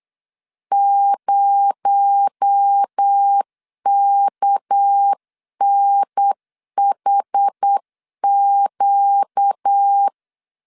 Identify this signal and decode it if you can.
Morse code, heard as a keyed tone, about 8 wpm, 800 Hz. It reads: 0KNHQ
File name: LEZIONI / 7 GF9Z / 7 G 8wpm 3118